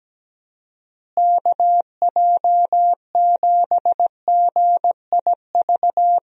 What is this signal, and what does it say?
Morse code, heard as a keyed tone, about 17 wpm, 700 Hz. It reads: KJ7GIV
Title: USA Call Signs - 6 Characters - Repeated 2x - 17wpm